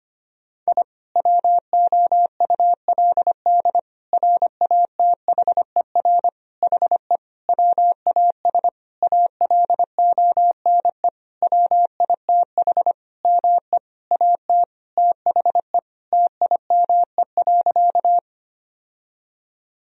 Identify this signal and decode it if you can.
Morse code, heard as a keyed tone, about 25 wpm, 690 Hz. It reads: I WOULD RATHER HE WAS ALONE WITH ME AT THE TIME.
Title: Sentences from Top 1000 Words - Review Part 4 - 25wpm